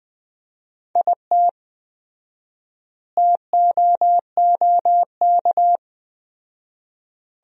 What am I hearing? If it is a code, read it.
Morse code, heard as a keyed tone, about 20 wpm, 700 Hz. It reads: IT TOOK
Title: 4x Word-Spacing - Sentences from Top 300 Words - 20wpm